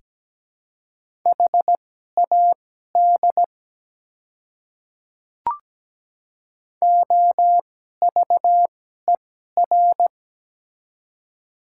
Morse code, heard as a keyed tone, about 17 wpm, spaced as for 12 wpm, 700 Hz. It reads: HAD OVER
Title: 2x Character Spacing - Top 200 Words - 17wpm